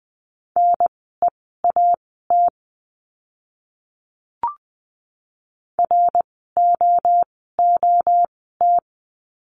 Morse code, heard as a keyed tone, about 20 wpm, spaced as for 14 wpm, 700 Hz. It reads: NEAT ROOT
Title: 061 Lesson 11f - Letter h - Words 20wpm